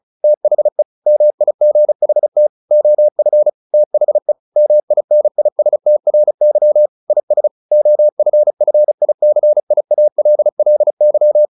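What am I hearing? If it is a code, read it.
Morse code, heard as a keyed tone, about 34 wpm, 600 Hz. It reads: THE MIGHT OF THE MINISTRY IS OFFICIALLY
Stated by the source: Morse practice at 35 wpm from CW med Gnister, sending Ken Follett's novel Never in English.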